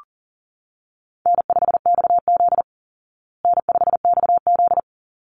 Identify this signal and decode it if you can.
Morse code, heard as a keyed tone, about 40 wpm, 700 Hz. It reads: N5XZ N5XZ
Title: Easy Call Signs Repeated 2x - 40wpm